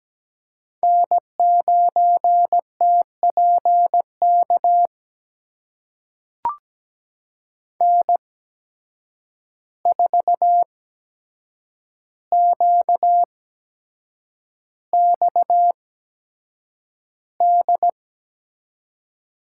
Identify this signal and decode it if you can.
Morse code, heard as a keyed tone, about 17 wpm, spaced as for 7 wpm, 700 Hz. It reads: N9TPK N4QXD
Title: USA Call Signs - 5 Characters - 8x Character Spacing - 17wpm